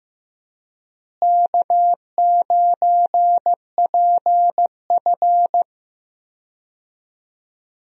Morse code, heard as a keyed tone, about 15 wpm, 700 Hz. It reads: K9PF
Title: USA Call Signs - 4 Characters - Repeated 3x - 15wpm